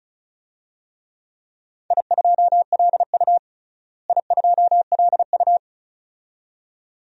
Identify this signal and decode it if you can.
Morse code, heard as a keyed tone, about 36 wpm, 700 Hz. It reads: I2LU I2LU
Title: DXCC-100 Call Signs - 4 Characters - Repeated 2x - 35wpm